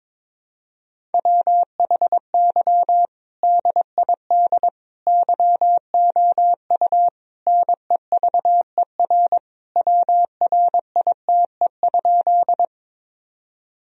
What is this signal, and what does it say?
Morse code, heard as a keyed tone, about 22 wpm, 700 Hz. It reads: WHY DID YOU NEVER WRITE?
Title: Sentences from Top 1000 Words - Review Part 1 - 22wpm